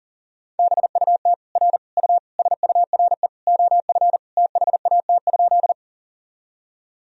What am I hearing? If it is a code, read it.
Morse code, heard as a keyed tone, about 37 wpm, 700 Hz. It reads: BUT R U SURE OF THAT?
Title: Sentences from Top 300 Words and Abbreviations 40wpm